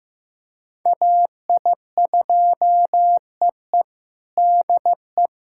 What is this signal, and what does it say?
Morse code, heard as a keyed tone, about 15 wpm, 700 Hz. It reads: AI2EE DE
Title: USA Callsign DE Callsign 15wpm